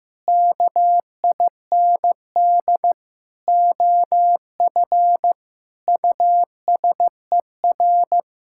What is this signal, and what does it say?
Morse code, heard as a keyed tone, about 15 wpm, 700 Hz. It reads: KIND OF USER
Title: Sentences from Top 1000 Words - Review Part 5 - 15wpm